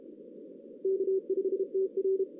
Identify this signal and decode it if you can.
Morse code, heard as a keyed tone, about 33 wpm, 380 Hz. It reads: K5TR
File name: 2016 arrl dx cw / 2016-ARRL-DX-CW-K5TR- - 0-3